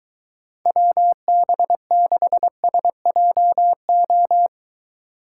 Morse code, heard as a keyed tone, about 22 wpm, 700 Hz. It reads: WB6SJO